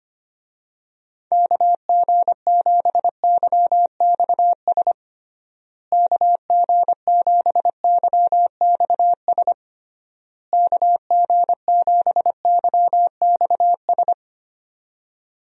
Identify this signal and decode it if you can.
Morse code, heard as a keyed tone, about 25 wpm, 700 Hz. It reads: KG7YXS KG7YXS KG7YXS
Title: USA Call Signs - 6 Characters - Repeated 3x - 25wpm